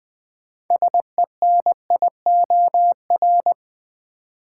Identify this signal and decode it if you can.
Morse code, heard as a keyed tone, about 20 wpm, 700 Hz. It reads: SENIOR
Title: Sets of 5 Words Encore - New Word Drill - 20wpm